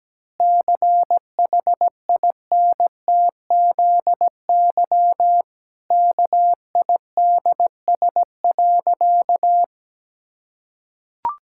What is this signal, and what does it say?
Morse code, heard as a keyed tone, about 17 wpm, 700 Hz. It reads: CHINTZY KIDS.
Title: Pangram Challenge v2 - 17wpm